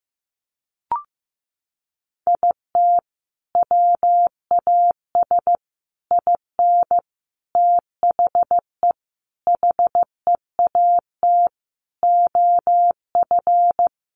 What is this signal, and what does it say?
Morse code, heard as a keyed tone, about 15 wpm, 700 Hz. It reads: IT WAS IN THE HEAT OF
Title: Sentences from Top 1300 Words 15wpm